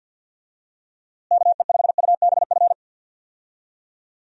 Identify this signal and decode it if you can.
Morse code, heard as a keyed tone, about 52 wpm, 700 Hz. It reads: KE5UBF
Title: USA Call Signs - 6 Characters - Repeated 2x - 50wpm